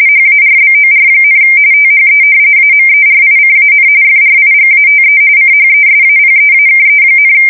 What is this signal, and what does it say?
RTTY, about 45.45 baud. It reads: HARBOR, THE AMERICAN FRIGATE CHESAPEAKE,
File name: fsk170 / usnavy2